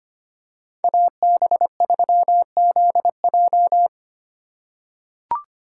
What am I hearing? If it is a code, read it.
Morse code, heard as a keyed tone, about 25 wpm, 700 Hz. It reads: AB3ZJ
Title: USA Call Signs - 5 Characters - 4x Character Spacing - 25wpm